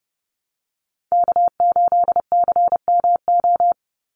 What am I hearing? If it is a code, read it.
Morse code, heard as a keyed tone, about 30 wpm, 700 Hz. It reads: K8CMO
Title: Call Signs Repeated 3x - 30wpm